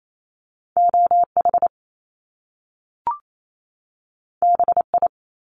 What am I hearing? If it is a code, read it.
Morse code, heard as a keyed tone, about 28 wpm, 700 Hz. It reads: OH BI
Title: Top Two Letter Combinations in Words 28wpm